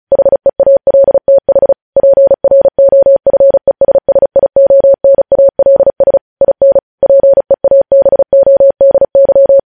Morse code, heard as a keyed tone, about 36 wpm, 550 Hz. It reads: HEALTH PROFESSIONALS IN PEABODY
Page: news headlines in morse code at 35 wpm